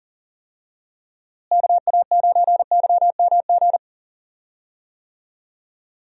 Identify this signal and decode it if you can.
Morse code, heard as a keyed tone, about 40 wpm, 700 Hz. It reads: KA9YMG
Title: USA Call Signs - 6 Characters - Repeated 3x - 40wpm